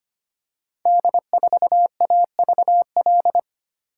Morse code, heard as a keyed tone, about 25 wpm, 700 Hz. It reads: D4AVL
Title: Code Groups - 5 Letter:Numbers - 25wpm